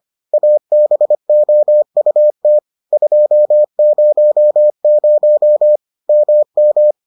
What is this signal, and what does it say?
Morse code, heard as a keyed tone, about 25 wpm, 600 Hz. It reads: ABOUT 200 MM